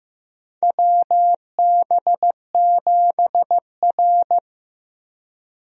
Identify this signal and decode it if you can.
Morse code, heard as a keyed tone, about 15 wpm, 700 Hz. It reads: WB7R
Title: USA Call Signs - 4 Characters - Repeated 2x - 15wpm